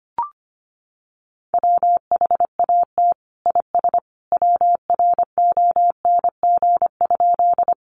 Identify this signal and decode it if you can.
Morse code, heard as a keyed tone, about 25 wpm, 700 Hz. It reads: WHAT IS WRONG?
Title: Sentences from Top 900 Words 25wpm